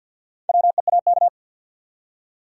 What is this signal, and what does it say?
Morse code, heard as a keyed tone, about 46 wpm, 700 Hz. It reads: WEAK
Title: Four Letter Words - 50wpm